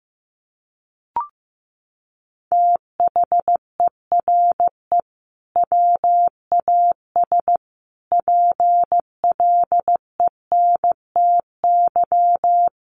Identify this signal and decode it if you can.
Morse code, heard as a keyed tone, about 15 wpm, 700 Hz. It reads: THERE WAS PLENTY
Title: Sentences from Top 1300 Words 15wpm